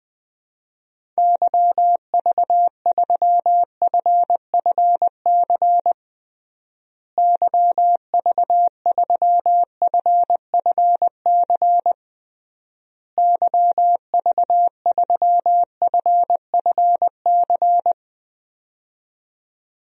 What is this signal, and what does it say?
Morse code, heard as a keyed tone, about 20 wpm, 700 Hz. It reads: YV3FFC YV3FFC YV3FFC
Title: DXCC-100 Call Signs - 6 Characters - Repeated 3x - 20wpm